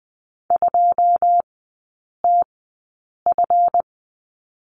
Morse code, heard as a keyed tone, about 20 wpm, spaced as for 8 wpm, 700 Hz. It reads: 2TF
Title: 152 Lesson 25e - Letter p - Three Characters 20wpm